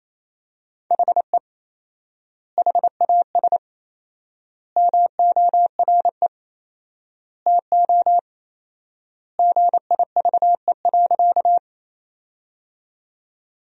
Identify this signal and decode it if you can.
Morse code, heard as a keyed tone, about 27 wpm, 700 Hz. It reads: HE HAS MORE TO GIVE.
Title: 4x Word-Spacing - Sentences from Top 200 Words - 28wpm